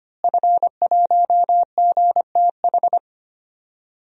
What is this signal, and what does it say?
Morse code, heard as a keyed tone, about 25 wpm, 700 Hz. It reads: F1GTH
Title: DXCC-100 Call Signs - 5 Characters - Repeated 3x - 25wpm